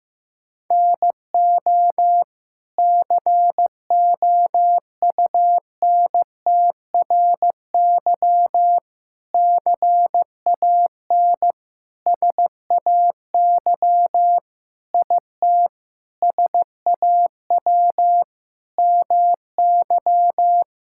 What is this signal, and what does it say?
Morse code, heard as a keyed tone, about 16 wpm, 700 Hz. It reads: NO COUNTRY CAN SAY IT SAW MY
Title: Sentences from Top 1000 Words - Review Part 5 - 15wpm